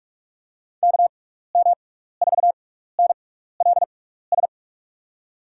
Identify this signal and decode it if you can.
Morse code, heard as a keyed tone, about 44 wpm, spaced as for 15 wpm, 700 Hz. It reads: KM4NLS
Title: USA Call Signs - 6 Characters - 6x Character Spacing - 45wpm